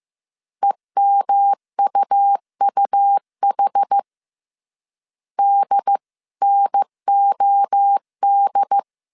Morse code, heard as a keyed tone, about 15 wpm, 800 Hz. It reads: EMUUH DNOD